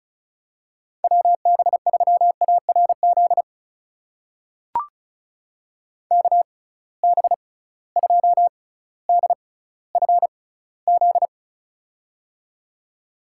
Morse code, heard as a keyed tone, about 36 wpm, spaced as for 18 wpm, 700 Hz. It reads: WB3ARZ KB2DFZ
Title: USA Call Signs - 6 Characters - 6x Character Spacing - 35wpm